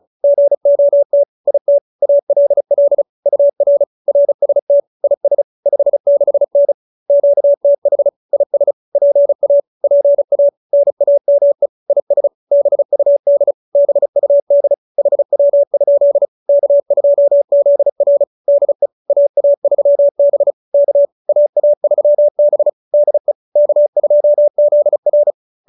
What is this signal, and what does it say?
Morse code, heard as a keyed tone, about 34 wpm, 570 Hz. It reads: GOT IT ALL UR RST IS 56N QTH IS PA PA NAME IS BUD BUD HW? K2ZR DE AA3B K AA3B DE K2ZR